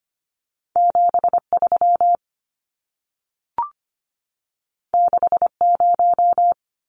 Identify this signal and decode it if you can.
Morse code, heard as a keyed tone, about 25 wpm, 700 Hz. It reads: 73 60
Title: Numbers - 2-Digit 25wpm